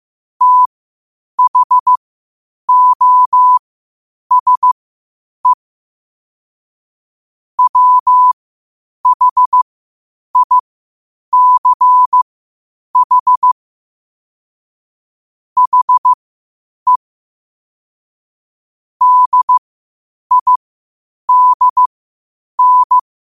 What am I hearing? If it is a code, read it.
Morse code, heard as a keyed tone, about 15 wpm, spaced as for 8 wpm, 1000 Hz. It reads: THOSE WHICH HE DIDN
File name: 8wpm